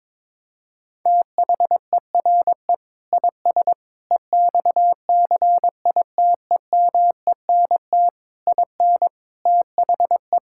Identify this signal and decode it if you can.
Morse code, heard as a keyed tone, about 22 wpm, 700 Hz. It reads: THERE IS EXCITEMENT IN THE
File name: practice / 22wpm / Sentences with 3-Word Phrases - 22wpm